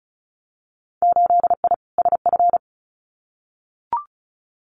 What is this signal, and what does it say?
Morse code, heard as a keyed tone, about 34 wpm, 700 Hz. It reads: 8I SF
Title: Winter Field Day Exchanges Encore 35wpm